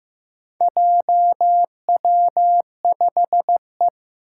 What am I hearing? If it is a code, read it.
Morse code, heard as a keyed tone, about 15 wpm, 700 Hz. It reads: JW5E
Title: Four Challenging Letter-Number Combinations 15wpm